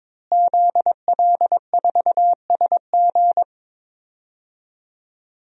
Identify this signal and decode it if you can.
Morse code, heard as a keyed tone, about 22 wpm, 700 Hz. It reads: ZL4SG
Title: DXCC-100 Call Signs - 5 Characters - Repeated 3x - 22wpm